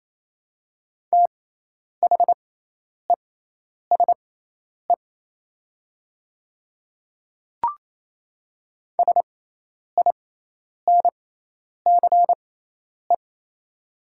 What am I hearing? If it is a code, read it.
Morse code, heard as a keyed tone, about 27 wpm, spaced as for 10 wpm, 700 Hz. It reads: THESE SINCE
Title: 6x Character Spacing - Top 300 Words - 28wpm